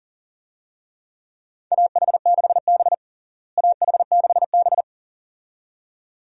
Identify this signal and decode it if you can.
Morse code, heard as a keyed tone, about 40 wpm, 700 Hz. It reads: AH6B AH6B
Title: USA Call Signs - 4 Characters - Repeated 2x - 40wpm